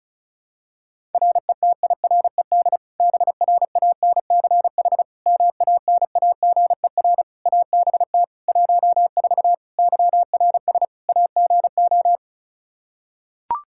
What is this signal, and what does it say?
Morse code, heard as a keyed tone, about 34 wpm, 700 Hz. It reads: RETIRED BRANCH MANAGER ABT 14 YRS AGO
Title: Practice QSO - Exchange 3 - More Info - 35wpm